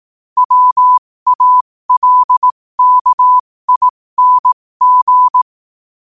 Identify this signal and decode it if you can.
Morse code, heard as a keyed tone, about 18 wpm, spaced as for 15 wpm, 1000 Hz. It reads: WALKING
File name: qotdfiles / 15wpm